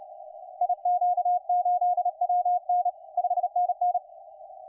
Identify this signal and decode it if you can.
Morse code, heard as a keyed tone, about 33 wpm, 720 Hz, with radio noise. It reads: IQ8WN 5NN
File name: 5T5OK 10CW